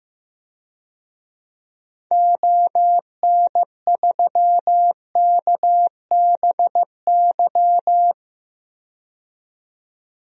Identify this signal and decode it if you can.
Morse code, heard as a keyed tone, about 15 wpm, 700 Hz. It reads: ON3KBY
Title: DXCC-100 Call Signs - 6 Characters - 15wpm